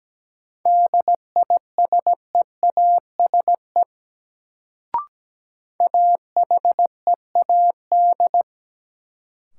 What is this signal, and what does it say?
Morse code, heard as a keyed tone, about 17 wpm, 700 Hz. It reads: DISEASE AHEAD
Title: Sentences from Top 1000 Words - Review Part 6 - New Word Drill - 17wpm